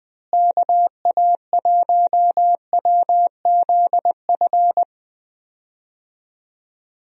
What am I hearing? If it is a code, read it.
Morse code, heard as a keyed tone, about 20 wpm, 700 Hz. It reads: KA1WZF
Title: USA Call Signs - 6 Characters - Repeated 3x - 20wpm